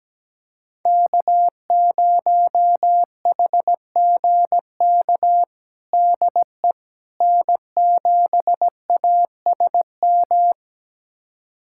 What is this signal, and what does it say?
Morse code, heard as a keyed tone, about 17 wpm, 700 Hz. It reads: K0HGK DE N7ASM
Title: USA Callsign DE Callsign - 5 Characters - 17wpm